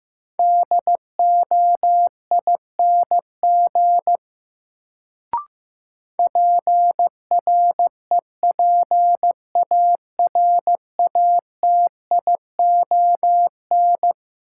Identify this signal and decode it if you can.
Morse code, heard as a keyed tone, about 15 wpm, 700 Hz. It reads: DOING PREPARATION
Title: Sentences from Top 1400 Words - New Word Drill Alt - 15wpm